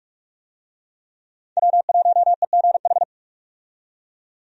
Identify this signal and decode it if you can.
Morse code, heard as a keyed tone, about 47 wpm, 700 Hz. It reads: W1EGH
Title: USA Call Signs - 5 Characters - 45wpm